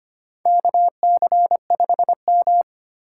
Morse code, heard as a keyed tone, about 25 wpm, 700 Hz. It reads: KC5M
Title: USA Call Signs - 4 Characters - 4x Character Spacing - 25wpm